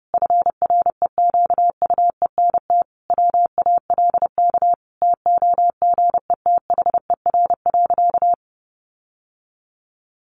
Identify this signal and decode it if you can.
Morse code, heard as a keyed tone, about 30 wpm, 700 Hz. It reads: FREQUENT WALK TOGETHER.
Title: Sentences from Top 1400 Words 30wpm